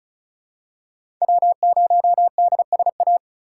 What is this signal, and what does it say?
Morse code, heard as a keyed tone, about 36 wpm, 700 Hz. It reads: W0DSA
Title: USA Call Signs - 5 Characters - 3x Character Spacing - 35wpm